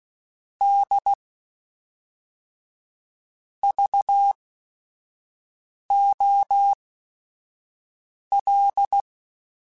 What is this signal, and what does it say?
Morse code, heard as a keyed tone, about 16 wpm, spaced as for 5 wpm, 800 Hz. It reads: D VOL
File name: Technical Subjects / Technical Subjects-mp3s / XL @5